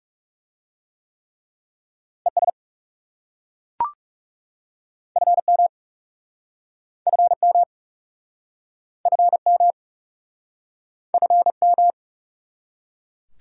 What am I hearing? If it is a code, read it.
Morse code, heard as a keyed tone, about 40 wpm, 700 Hz. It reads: ES FM FM FM FM